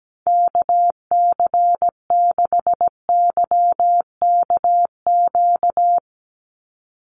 Morse code, heard as a keyed tone, about 17 wpm, 700 Hz. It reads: KC6YKQ
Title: USA Call Signs 17wpm